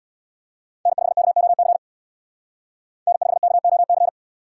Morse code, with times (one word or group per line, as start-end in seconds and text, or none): 0.9-1.8 N5KOC
3.1-4.1 N5KOC